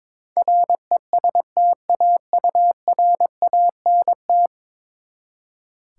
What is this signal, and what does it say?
Morse code, heard as a keyed tone, about 21 wpm, 700 Hz. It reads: RESTAURANT
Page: morse-code-ninja-practice-22wpm